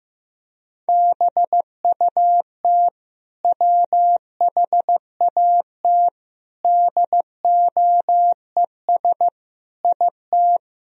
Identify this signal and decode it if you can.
Morse code, heard as a keyed tone, about 15 wpm, 700 Hz. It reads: BUT WHAT DOES IT
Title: Sentences from Top 1000 Words - Review Part 4 - 15wpm